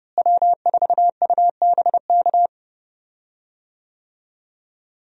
Morse code, heard as a keyed tone, about 31 wpm, 700 Hz. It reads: W4UBK
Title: USA Call Signs - 5 Characters - Repeated 3x - 30wpm